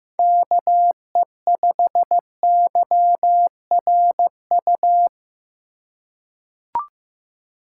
Morse code, heard as a keyed tone, about 15 wpm, 700 Hz. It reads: KE5YRU
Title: USA Call Signs - 6 Characters - 2x Character Spacing - 15wpm